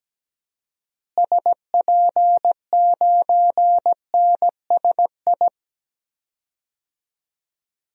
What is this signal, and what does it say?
Morse code, heard as a keyed tone, about 17 wpm, 700 Hz. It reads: SP9NSI
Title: DXCC-100 Call Signs - 6 Characters - Repeated 2x - 17wpm